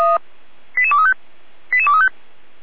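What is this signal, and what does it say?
Five-tone selective call (ZVEI), heard as quick runs of five tones, tones of about 80 ms. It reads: 88235 88235
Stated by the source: ZVEI five-tone selective call received by SDR.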